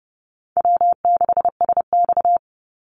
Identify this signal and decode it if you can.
Morse code, heard as a keyed tone, about 29 wpm, 700 Hz. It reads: W6SX
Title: Easy Call Signs Repeated 2x - 30wpm